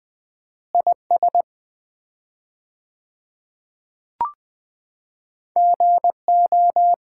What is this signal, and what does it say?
Morse code, heard as a keyed tone, about 20 wpm, 700 Hz. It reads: IS GO
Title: 4x Word-Spacing - Sets of 3 Words - 20wpm